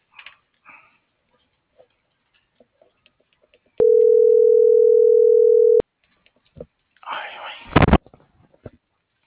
Ringback tone (North America), 440 + 480 Hz.